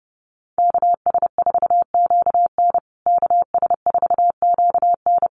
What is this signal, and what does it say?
Morse code, heard as a keyed tone, about 30 wpm, 700 Hz. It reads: KS4QN KS4QN
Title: USA Call Signs Repeated 3x 30wpm